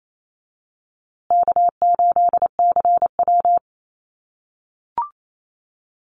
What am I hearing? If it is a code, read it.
Morse code, heard as a keyed tone, about 28 wpm, 700 Hz. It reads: K8CW
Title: USA Call Signs Repeated 3x 28wpm